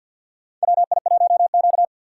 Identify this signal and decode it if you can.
Morse code, heard as a keyed tone, about 45 wpm, 700 Hz. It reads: WI1Q
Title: USA Call Signs - 4 Characters - 2x Character Spacing - 50wpm